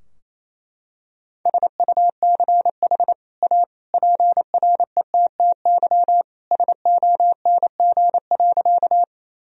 Morse code, heard as a keyed tone, about 28 wpm, 700 Hz. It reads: SUCH A PRETTY SONG.
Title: Sentences from Top 1000 Words - Review Part 2 - 28wpm